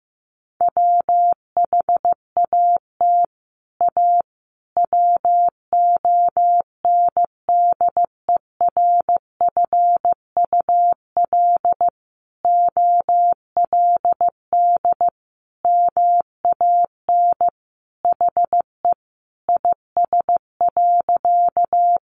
Morse code, heard as a keyed tone, about 15 wpm, 700 Hz. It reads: WHAT A WONDERFUL OLD MAN HE IS.
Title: Sentences from Top 1100 Words 15wpm